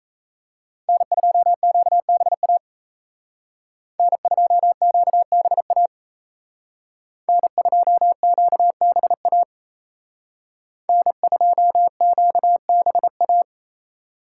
Morse code, heard as a keyed tone, about 37 wpm, 700 Hz. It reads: N2QBA N2QBA N2QBA N2QBA